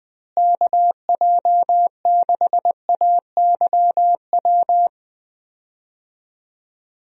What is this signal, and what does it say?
Morse code, heard as a keyed tone, about 19 wpm, 700 Hz. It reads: KJ6AYW
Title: USA Call Signs - 6 Characters - Repeated 3x - 20wpm